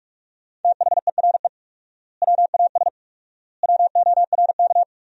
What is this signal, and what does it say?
Morse code, heard as a keyed tone, about 42 wpm, 700 Hz. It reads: THERE WAS WORK
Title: 4x Word-Spacing - Sentences from Top 200 Words - 45wpm